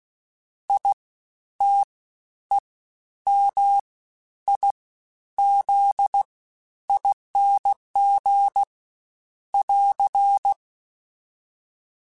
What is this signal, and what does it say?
Morse code, heard as a keyed tone, about 16 wpm, 800 Hz. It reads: I T E M I Z ING <AR>
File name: Special INGs-mp3s / Special INGs-mp3s / SlowToFaster ing @15